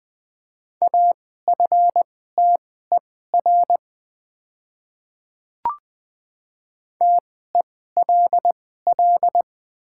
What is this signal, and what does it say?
Morse code, heard as a keyed tone, about 21 wpm, spaced as for 15 wpm, 700 Hz. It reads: AFTER TELL
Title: 2x Character Spacing - Top 200 Words - 20wpm